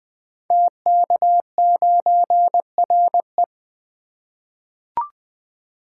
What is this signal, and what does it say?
Morse code, heard as a keyed tone, about 20 wpm, 700 Hz. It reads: TK9RE
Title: Code Groups - 5 Letter:Numbers - 20wpm